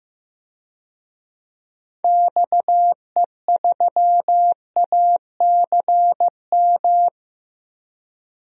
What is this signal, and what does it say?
Morse code, heard as a keyed tone, about 16 wpm, 700 Hz. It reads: XE3ACM
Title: DXCC-100 Call Signs - 6 Characters - Repeated 3x - 15wpm